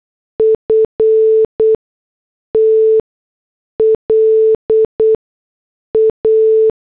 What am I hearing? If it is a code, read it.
Morse code, heard as a keyed tone, about 8 wpm, spaced as for 6 wpm, 440 Hz. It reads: FT LA
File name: morse / headlines